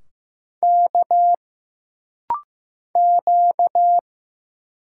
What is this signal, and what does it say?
Morse code, heard as a keyed tone, about 15 wpm, 700 Hz. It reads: K Q
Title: Single Letters - Rapid-Fire 15wpm